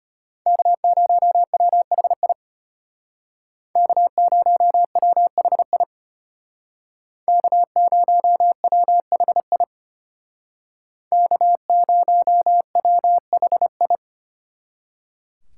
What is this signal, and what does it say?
Morse code, heard as a keyed tone, about 33 wpm, 700 Hz. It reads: K0WHI K0WHI K0WHI K0WHI